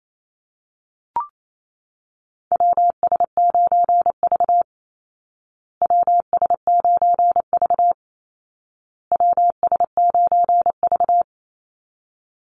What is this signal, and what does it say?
Morse code, heard as a keyed tone, about 28 wpm, 700 Hz. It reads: WS9V WS9V WS9V
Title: Call Signs Repeated 2x - 28wpm